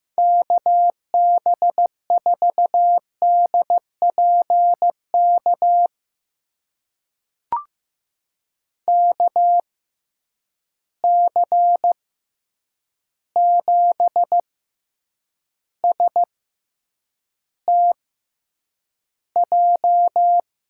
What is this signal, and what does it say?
Morse code, heard as a keyed tone, about 15 wpm, spaced as for 8 wpm, 700 Hz. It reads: KB4DPK KC7STJ
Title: USA Call Signs - 6 Characters - 6x Character Spacing - 15wpm